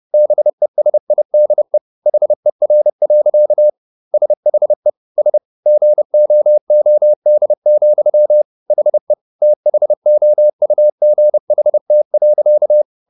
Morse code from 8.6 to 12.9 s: HE THOUGHT.

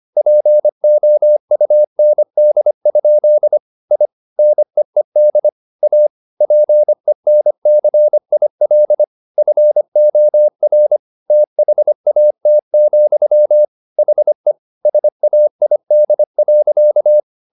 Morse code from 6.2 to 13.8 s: PENCIL FOR THAT,

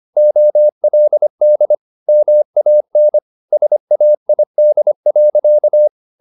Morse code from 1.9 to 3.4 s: MAN

